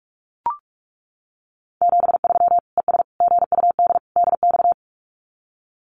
Morse code, heard as a keyed tone, about 44 wpm, 700 Hz. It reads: 73 ES GUD DX
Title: Common QSO Phrases 45wpm